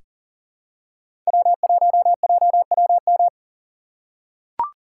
Morse code, heard as a keyed tone, about 40 wpm, 700 Hz. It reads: W1JWM
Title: USA Call Signs - 6x Character Spacing - 40wpm